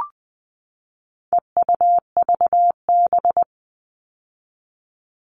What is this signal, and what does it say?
Morse code, heard as a keyed tone, about 20 wpm, 700 Hz. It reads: EUVB
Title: All Cut Numbers - 4-Digit 20wpm